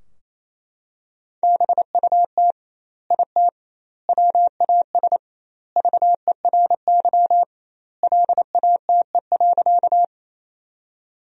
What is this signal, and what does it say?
Morse code, heard as a keyed tone, about 27 wpm, 700 Hz. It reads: BUT IT WAS VERY LATE.
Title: 2x Word-Spacing - Sentences from Top 300 Words - 28wpm